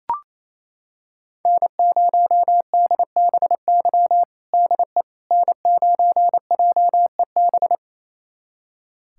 Morse code, heard as a keyed tone, about 28 wpm, 700 Hz. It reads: N0DBY DE N9JEB
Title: USA Callsign DE Callsign 28wpm